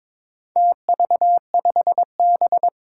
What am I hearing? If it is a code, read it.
Morse code, heard as a keyed tone, about 22 wpm, 700 Hz. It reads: TV5B
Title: Four Challenging Letter-Number Combinations 22wpm